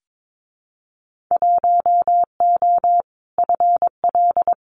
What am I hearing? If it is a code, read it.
Morse code, heard as a keyed tone, about 22 wpm, 700 Hz. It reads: 1O FL
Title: Winter Field Day Exchanges 22wpm